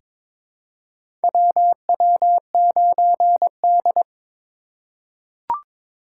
Morse code, heard as a keyed tone, about 23 wpm, 700 Hz. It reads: WW9D
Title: USA Call Signs - 4 Characters - 4x Character Spacing - 22wpm